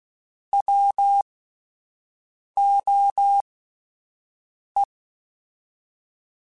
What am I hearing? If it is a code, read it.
Morse code, heard as a keyed tone, about 16 wpm, spaced as for 5 wpm, 800 Hz. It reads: WOE
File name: Alphabetical / Alphabetical mp3s / Ws @5